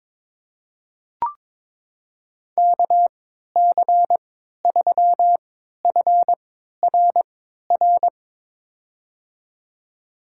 Morse code, heard as a keyed tone, about 22 wpm, spaced as for 13 wpm, 700 Hz. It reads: KC3FRR
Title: USA Call Signs - 3x Character Spacing - 22wpm